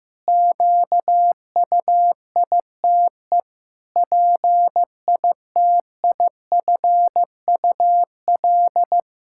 Morse code, heard as a keyed tone, about 16 wpm, 700 Hz. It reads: QUITE PITIFUL
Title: Pangram Challenge v2 - 15wpm